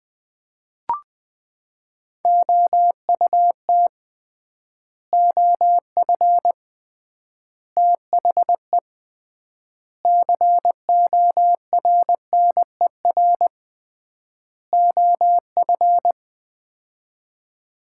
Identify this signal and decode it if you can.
Morse code, heard as a keyed tone, about 20 wpm, 700 Hz. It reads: OUT OF THE CORNER OF
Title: 3x Word-Spacing - Sets of 5 Words - 20wpm